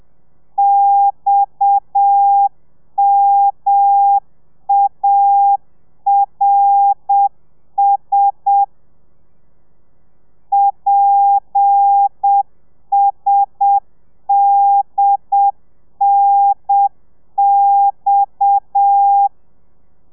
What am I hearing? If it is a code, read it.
Morse code, heard as a keyed tone, about 7 wpm, 800 Hz. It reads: XMARS PSDNX
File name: telegrafia / i2tfj / Lez7 30